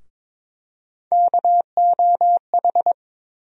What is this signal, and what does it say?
Morse code, heard as a keyed tone, about 22 wpm, 700 Hz. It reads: KOH
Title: USA Call Sign Postfixes - 3 Letters 22wpm